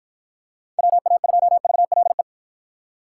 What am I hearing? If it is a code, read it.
Morse code, heard as a keyed tone, about 52 wpm, 700 Hz. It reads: WA2VLE